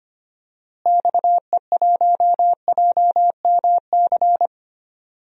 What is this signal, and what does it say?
Morse code, heard as a keyed tone, about 25 wpm, 700 Hz. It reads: XE1JMC
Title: DXCC-100 Call Signs - 6 Characters - Repeated 2x - 25wpm